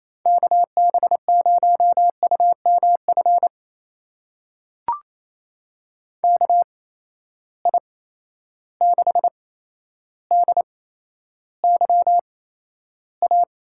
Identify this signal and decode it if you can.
Morse code, heard as a keyed tone, about 27 wpm, spaced as for 12 wpm, 700 Hz. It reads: KB0UMF KI6DYA
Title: USA Call Signs - 6 Characters - 8x Character Spacing - 28wpm